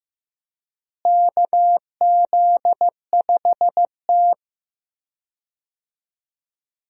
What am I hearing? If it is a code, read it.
Morse code, heard as a keyed tone, about 15 wpm, 700 Hz. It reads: KZ5T
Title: USA Call Signs - 4 Characters - Repeated 2x - 15wpm